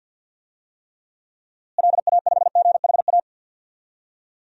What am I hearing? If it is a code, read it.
Morse code, heard as a keyed tone, about 52 wpm, 700 Hz. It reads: LA5GHA
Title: DXCC-100 Call Signs - 6 Characters - Repeated 3x - 50wpm